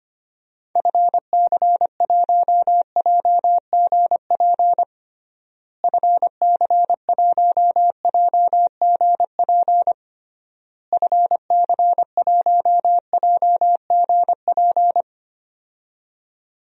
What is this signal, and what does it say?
Morse code, heard as a keyed tone, about 25 wpm, 700 Hz. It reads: FC1JGP FC1JGP FC1JGP
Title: DXCC-100 Call Signs - 6 Characters - Repeated 3x - 25wpm